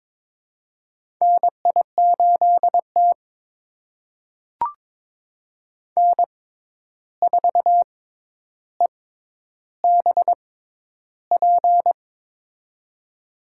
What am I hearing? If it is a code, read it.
Morse code, heard as a keyed tone, about 22 wpm, spaced as for 11 wpm, 700 Hz. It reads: NI8T N4EBP
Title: USA Call Signs - 6x Character Spacing - 22wpm